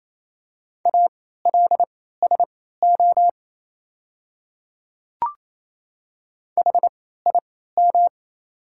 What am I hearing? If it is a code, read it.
Morse code, heard as a keyed tone, about 27 wpm, spaced as for 16 wpm, 700 Hz. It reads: ALSO HIM
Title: 3x Character Spacing - Top 200 Words - 28wpm